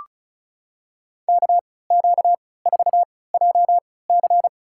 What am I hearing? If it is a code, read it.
Morse code, heard as a keyed tone, about 36 wpm, spaced as for 20 wpm, 700 Hz. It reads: KQ4JC
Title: USA Call Signs - 5 Characters - 3x Character Spacing - 35wpm